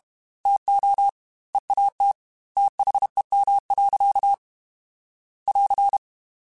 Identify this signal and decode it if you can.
Morse code, heard as a keyed tone, about 32 wpm, 800 Hz. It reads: TO EAT THEM. <AR>
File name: Chuckles / Chuckles-mp3s / SoftWords @30